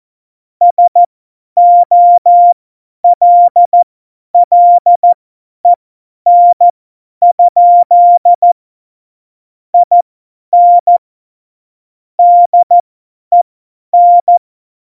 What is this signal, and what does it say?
Morse code, heard as a keyed tone, about 14 wpm, spaced as for 10 wpm, 700 Hz. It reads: SOLLEN? IN DEN
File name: Morsekurs / Witze / Witz 22. September 2020, 50 F70